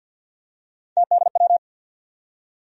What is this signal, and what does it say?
Morse code, heard as a keyed tone, about 49 wpm, 700 Hz. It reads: TDW
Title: USA Call Sign Postfixes - 3 Letters 50wpm